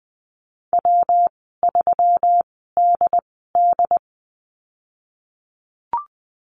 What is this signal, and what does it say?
Morse code, heard as a keyed tone, about 20 wpm, spaced as for 15 wpm, 700 Hz. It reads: W3DD
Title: 128 Lesson 21g - Number 6 - Call Signs 20wpm